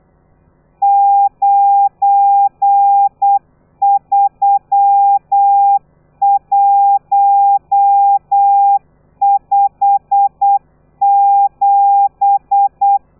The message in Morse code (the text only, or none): 93157